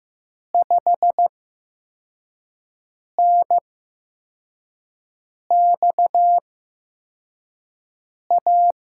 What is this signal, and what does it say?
Morse code, heard as a keyed tone, about 15 wpm, spaced as for 4 wpm, 700 Hz. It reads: 5NXA
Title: USA Call Signs - 6 Characters - 8x Character Spacing - 15wpm